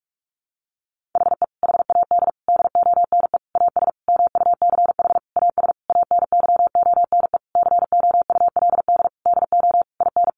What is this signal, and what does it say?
Morse code, heard as a keyed tone, about 44 wpm, 700 Hz. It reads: HE HAD DONE AS MUCH AS ANYONE COULD DO IN
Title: Sentences from Top 1200 Words 45wpm